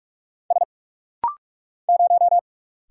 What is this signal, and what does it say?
Morse code, heard as a keyed tone, about 46 wpm, 700 Hz. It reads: S 0
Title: Single Letter-Number - Rapid-Fire 45wpm